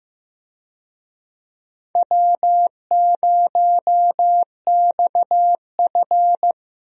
Morse code, heard as a keyed tone, about 16 wpm, 700 Hz. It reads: W0XF
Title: USA Call Signs - 4 Characters - Repeated 3x - 15wpm